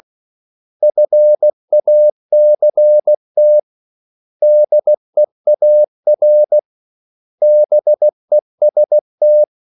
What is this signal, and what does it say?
Morse code, heard as a keyed tone, about 16 wpm, 600 Hz. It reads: FACT DEAR BEST
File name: CW / Practice / 500MostCommonEnglishWords16